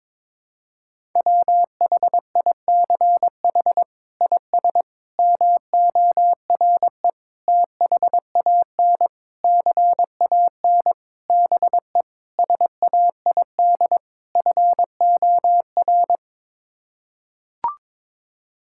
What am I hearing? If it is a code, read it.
Morse code, heard as a keyed tone, about 21 wpm, 700 Hz. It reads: WHICH IS MORE THAN CAN BE SAID FOR
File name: practice / 22wpm / Sets of 8 Words - 22wpm